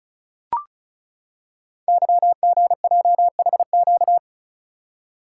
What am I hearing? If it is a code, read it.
Morse code, heard as a keyed tone, about 36 wpm, 690 Hz. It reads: YGJHQ
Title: Code Groups - 5 Letters - 35wpm